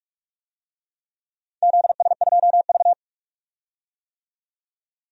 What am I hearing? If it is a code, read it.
Morse code, heard as a keyed tone, about 46 wpm, 700 Hz. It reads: ZS2V
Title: DXCC-100 Call Signs - 4 Characters - 45wpm